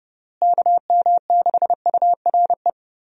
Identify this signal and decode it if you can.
Morse code, heard as a keyed tone, about 32 wpm, 700 Hz. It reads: KM6URE
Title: USA Call Signs - 6 Characters - Repeated 3x - 30wpm